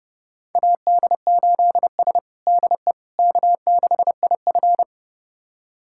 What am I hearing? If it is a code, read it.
Morse code, heard as a keyed tone, about 32 wpm, 700 Hz. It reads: AD8S DE K6IF
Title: USA Callsign DE Callsign - 4 Characters - 30wpm